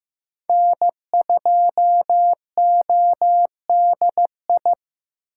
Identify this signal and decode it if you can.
Morse code, heard as a keyed tone, about 15 wpm, 700 Hz. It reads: N2ODI
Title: USA Call Signs - 5 Characters - 6x Character Spacing - 15wpm